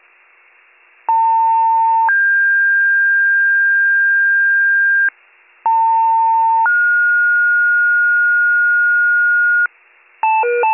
Two-tone page, heard as a one-second tone, then a three-second tone, about 900 then 1600 Hz, about 900 then 1430 Hz, with noise. It is heard from 1.1 to 9.7 s.